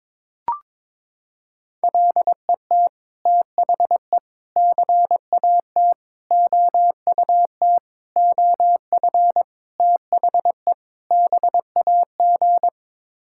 Morse code, heard as a keyed tone, about 21 wpm, 700 Hz. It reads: LET THE CAT OUT OF THE BAG
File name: practice / 22wpm / Sets of 7 Words - 22wpm